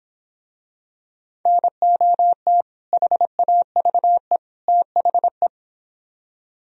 Morse code, heard as a keyed tone, about 26 wpm, 700 Hz. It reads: NOT HAVE THE